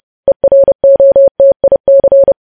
Morse code, heard as a keyed tone, about 30 wpm, 550 Hz. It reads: EROTIC
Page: news headlines in morse code at 30 wpm